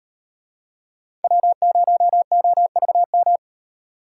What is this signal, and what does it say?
Morse code, heard as a keyed tone, about 37 wpm, 700 Hz. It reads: W0OVM